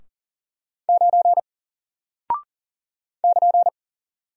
Morse code, heard as a keyed tone, about 40 wpm, 700 Hz. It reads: 9 <KN>
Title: Single Character (Letter:Number:Punctuation:Prosign) Alt 40wpm